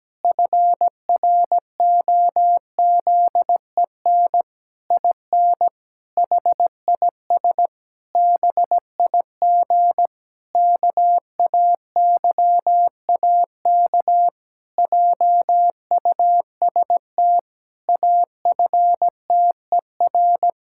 Morse code, heard as a keyed tone, about 17 wpm, 700 Hz. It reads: FROZEN IN HIS BIG KAYAK JUST AFTER
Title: Pangram Challenge v2 - 17wpm